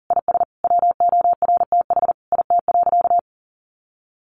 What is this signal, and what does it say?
Morse code, heard as a keyed tone, about 40 wpm, 700 Hz. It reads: IS WORTH IT.
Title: Sentences from Top 800 Words 40wpm